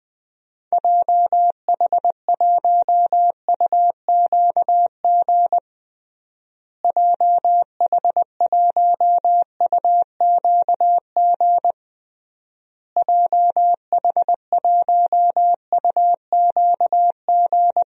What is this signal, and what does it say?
Morse code, heard as a keyed tone, about 20 wpm, 700 Hz. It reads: JH1UQG JH1UQG JH1UQG
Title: DXCC-100 Call Signs - 6 Characters - Repeated 3x - 20wpm